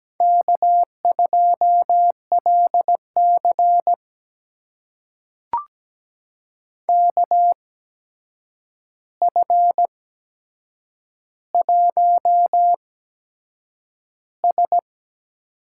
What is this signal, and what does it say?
Morse code, heard as a keyed tone, about 17 wpm, spaced as for 7 wpm, 700 Hz. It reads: K2LC KF1S
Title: USA Call Signs - 4 Characters - 8x Character Spacing - 17wpm